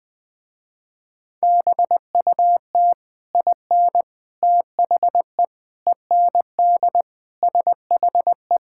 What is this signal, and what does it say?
Morse code, heard as a keyed tone, about 20 wpm, 700 Hz. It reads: BUT IN THE END SHE
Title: Sentences with 3-Word Phrases - Spoken Beginning - 20wpm